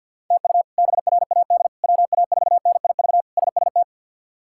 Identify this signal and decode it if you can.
Morse code, heard as a keyed tone, about 52 wpm, 700 Hz. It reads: TU BRAD WA4NIV SST
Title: K1USN SST QSO Practice - 50wpm